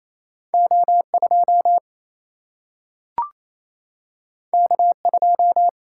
Morse code, heard as a keyed tone, about 28 wpm, 700 Hz. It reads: O 2 K 2
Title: Single-Letter Single-Number 28wpm